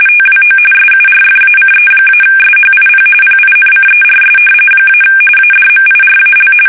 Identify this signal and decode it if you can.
RTTY, about 45.45 baud. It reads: TWO SURFACE-SHIP PINCERS - ADMIRAL
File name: fsk850 / battleshipnj